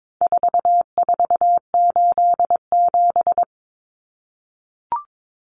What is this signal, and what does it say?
Morse code, heard as a keyed tone, about 22 wpm, 700 Hz. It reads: 4487
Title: Numbers - 4-Digit 22wpm